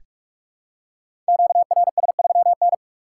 Morse code, heard as a keyed tone, about 44 wpm, 700 Hz. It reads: QRS3N
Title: Code Groups - 5 Letter:Numbers - 45wpm